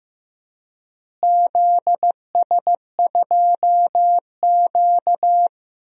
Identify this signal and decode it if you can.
Morse code, heard as a keyed tone, about 15 wpm, 700 Hz. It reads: ZS2Q
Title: DXCC-100 Call Signs - 4 Characters - Repeated 3x - 15wpm